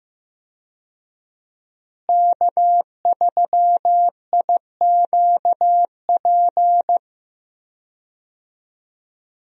Morse code, heard as a keyed tone, about 15 wpm, 700 Hz. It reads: K3IQP
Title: USA Call Signs - 5 Characters - Repeated 3x - 15wpm